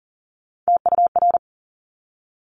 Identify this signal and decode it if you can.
Morse code, heard as a keyed tone, about 40 wpm, 690 Hz. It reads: TUR
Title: Top Three Letter Combinations in Words 40wpm